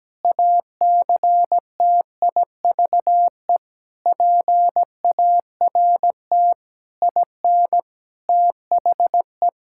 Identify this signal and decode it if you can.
Morse code, heard as a keyed tone, about 17 wpm, 700 Hz. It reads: ACTIVE PART IN THE
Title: Sentences from Top 1000 Words - Review Part 6 - 17wpm